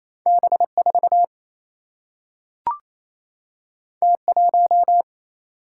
Morse code, heard as a keyed tone, about 28 wpm, 700 Hz. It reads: B4 T1
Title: Single-Letter Single-Number 28wpm